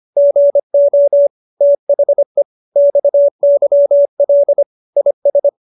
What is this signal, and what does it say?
Morse code, heard as a keyed tone, about 25 wpm, 570 Hz. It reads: GO THE XYL IS